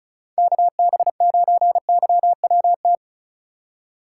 Morse code, heard as a keyed tone, about 36 wpm, 700 Hz. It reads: KB9YWT
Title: USA Call Signs - 4x Character Spacing - 35wpm